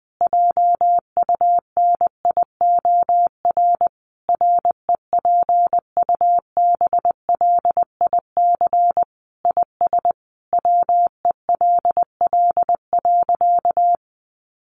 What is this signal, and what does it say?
Morse code, heard as a keyed tone, about 20 wpm, 700 Hz. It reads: JUNIOR REPUBLIC IS WELL.
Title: Sentences from Top 1500 Words 20wpm